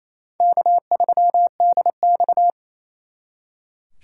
Morse code, heard as a keyed tone, about 35 wpm, 700 Hz. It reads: K3DX
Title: Speed Racing - USA 1x2 Call Signs 28wpm